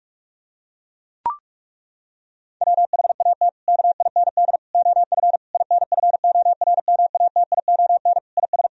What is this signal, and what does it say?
Morse code, heard as a keyed tone, about 41 wpm, 700 Hz. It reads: WHAT KIND OF INFORMATION IS
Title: Sentences from Top 1000 Words - Review Part 3 - 45wpm